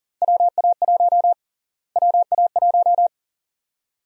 Morse code, heard as a keyed tone, about 40 wpm, 700 Hz. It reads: WA1 WA1
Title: USA Call Sign Prefixes - Repeated 2x - 40wpm